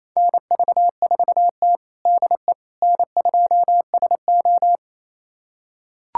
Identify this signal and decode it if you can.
Morse code, heard as a keyed tone, about 28 wpm, 690 Hz. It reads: NV4T DE N2SO
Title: USA Callsign DE Callsign - 4 Characters - 28wpm